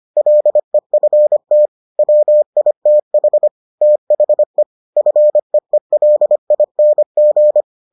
Morse code, heard as a keyed tone, about 25 wpm, 600 Hz. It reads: LEFT WITH THE FEELING